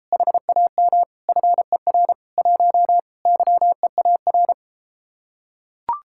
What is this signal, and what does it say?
Morse code, heard as a keyed tone, about 33 wpm, 700 Hz. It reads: HAM FER 1 YEAR